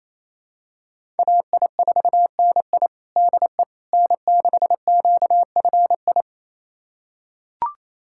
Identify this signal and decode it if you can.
Morse code, heard as a keyed tone, about 28 wpm, 700 Hz. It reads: AI4NI DE N6QFI
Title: USA Callsign DE Callsign - 5 Characters - 28wpm